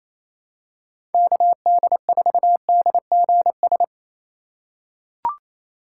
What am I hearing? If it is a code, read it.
Morse code, heard as a keyed tone, about 28 wpm, 700 Hz. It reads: KD4DGS
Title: USA Call Signs - 6 Characters - 4x Character Spacing - 28wpm